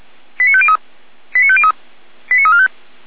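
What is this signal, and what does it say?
Five-tone selective call (ZVEI), heard as quick runs of five tones, tones of about 70 ms. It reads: 89503 89503 89345
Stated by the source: ZVEI five-tone selective call 89097, received on SDR, from feuerwehr-orlamuende fire brigade.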